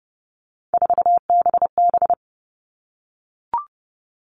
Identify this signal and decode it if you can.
Morse code, heard as a keyed tone, about 30 wpm, 700 Hz. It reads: 4BB
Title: All Cut Numbers - 3-Digit 30wpm